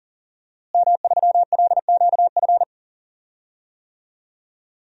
Morse code, heard as a keyed tone, about 40 wpm, 700 Hz. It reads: M3LQF
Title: DXCC-100 Call Signs - 5 Characters - Repeated 2x - 40wpm